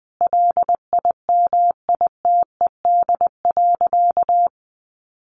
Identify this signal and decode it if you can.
Morse code, heard as a keyed tone, about 20 wpm, 700 Hz. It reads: LIMITED.
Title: Sentences from Top 900 Words 20wpm